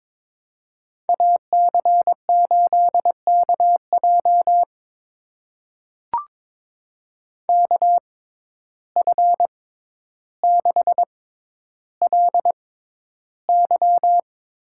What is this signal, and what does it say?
Morse code, heard as a keyed tone, about 23 wpm, spaced as for 11 wpm, 700 Hz. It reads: AC8KJ KF6LY
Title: USA Call Signs - 5 Characters - 6x Character Spacing - 22wpm